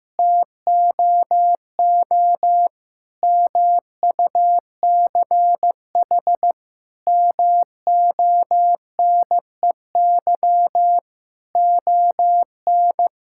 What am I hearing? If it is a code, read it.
Morse code, heard as a keyed tone, about 15 wpm, 700 Hz. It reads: TOO MUCH MONEY ON
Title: Sentences from Top 1000 Words - Review Part 5 - 15wpm